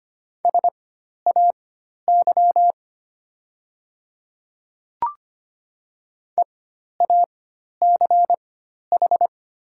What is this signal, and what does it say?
Morse code, heard as a keyed tone, about 25 wpm, spaced as for 12 wpm, 700 Hz. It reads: SAY EACH
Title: 4x Character Spacing - Top 200 Words - 25wpm